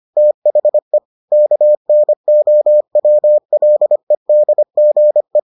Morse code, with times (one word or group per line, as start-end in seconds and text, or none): 0.2-1.0 THE
1.3-5.4 KNOWLEDGE